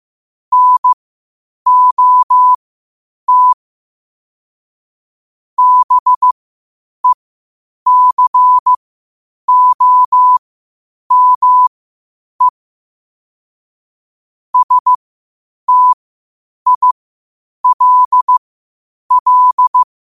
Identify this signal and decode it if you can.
Morse code, heard as a keyed tone, about 15 wpm, spaced as for 8 wpm, 1000 Hz. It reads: NOT BECOME STILL